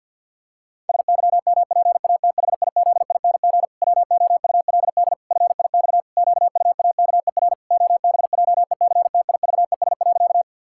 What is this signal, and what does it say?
Morse code, heard as a keyed tone, about 50 wpm, 700 Hz. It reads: SYMPATHIZING WOULD FIX QUAKER OBJECTIVES.
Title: Pangram Challenge v2 - 50wpm